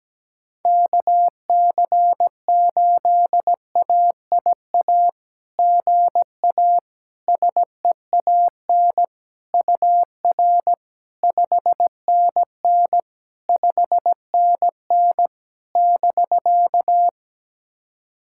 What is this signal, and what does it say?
Morse code, heard as a keyed tone, about 17 wpm, 700 Hz. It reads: KC8AIA GA SEAN UR 5NN 5NN <BK>
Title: SOTA QSO Practice 17wpm